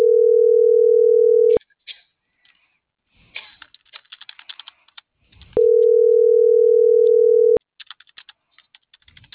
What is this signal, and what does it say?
Ringback tone (North America), 440 + 480 Hz.